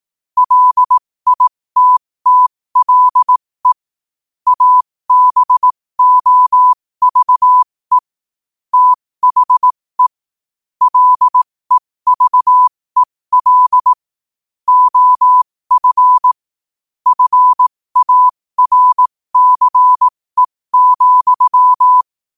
Morse code, heard as a keyed tone, about 18 wpm, spaced as for 15 wpm, 1000 Hz. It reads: LITTLE ABOVE THE LEVEL OF FARCE,
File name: qotdfiles / 15wpm